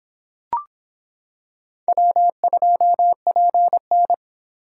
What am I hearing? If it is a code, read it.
Morse code, heard as a keyed tone, about 26 wpm, 700 Hz. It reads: W2PN